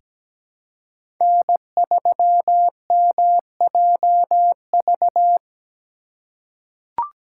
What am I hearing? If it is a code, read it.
Morse code, heard as a keyed tone, about 17 wpm, 700 Hz. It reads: N3MJV
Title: USA Call Signs - 5 Characters - 4x Character Spacing - 17wpm